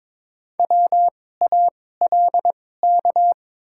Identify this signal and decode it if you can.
Morse code, heard as a keyed tone, about 22 wpm, spaced as for 16 wpm, 690 Hz. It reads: WALK
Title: 2x Character Spacing - Top 500 Words - 22wpm